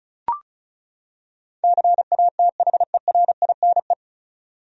Morse code, heard as a keyed tone, about 34 wpm, 700 Hz. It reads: CATHERINE
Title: USA Common Names - 35wpm